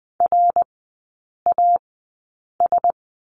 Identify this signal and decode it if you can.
Morse code, heard as a keyed tone, about 20 wpm, spaced as for 8 wpm, 700 Hz. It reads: RAS
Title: 053 Lesson 10e - Letter r - Three Characters 20wpm